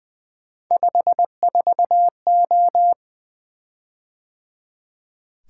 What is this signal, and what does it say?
Morse code, heard as a keyed tone, about 20 wpm, 700 Hz. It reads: 54O
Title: Three Letter-Number Combinations - 20wpm